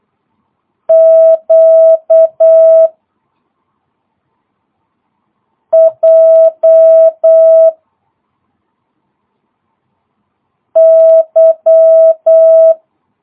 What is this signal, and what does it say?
Morse code, heard as a keyed tone, about 8 wpm, 650 Hz. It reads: Q J Y
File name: jh8cbh / 2022cw